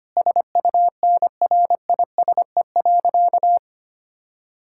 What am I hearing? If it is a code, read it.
Morse code, heard as a keyed tone, about 25 wpm, 700 Hz. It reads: SUNRISE.
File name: practice / 25wpm / Difficult Sentences with Hints - 25wpm